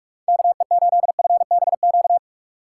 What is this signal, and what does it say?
Morse code, heard as a keyed tone, about 45 wpm, 700 Hz. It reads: KE8FBQ
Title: USA Call Signs - 6 Characters - 2x Character Spacing - 45wpm